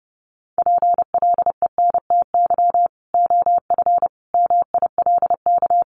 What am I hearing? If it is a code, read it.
Morse code, heard as a keyed tone, about 28 wpm, 700 Hz. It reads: PLENTY OF MILK
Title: Sentences from Top 1500 Words 30wpm